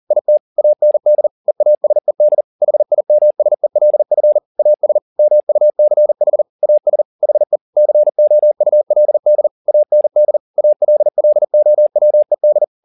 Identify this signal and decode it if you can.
Morse code, heard as a keyed tone, about 40 wpm, 600 Hz. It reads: IT AND EASED HIMSELF AS MUCH AS HE COULD AND ALLOWED
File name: the old man and the sea 2 40wpm